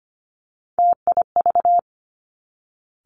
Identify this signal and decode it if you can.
Morse code, heard as a keyed tone, about 25 wpm, 700 Hz. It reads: TIV
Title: Three Challenging Letter-Number Combinations 25wpm